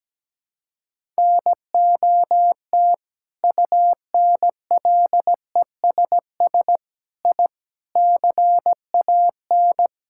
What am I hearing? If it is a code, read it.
Morse code, heard as a keyed tone, about 17 wpm, 700 Hz. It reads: NOT UNLESS I CAN
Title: Sentences from Top 1000 Words - Review Part 6 - 17wpm